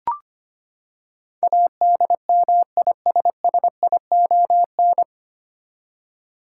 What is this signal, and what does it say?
Morse code, heard as a keyed tone, about 25 wpm, 700 Hz. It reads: ADMISSION
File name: Nine Letter Words - 25wpm